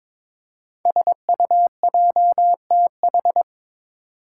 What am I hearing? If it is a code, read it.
Morse code, heard as a keyed tone, about 22 wpm, 690 Hz. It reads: SUJTH
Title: Code Groups - 5 Letter:Numbers - 22wpm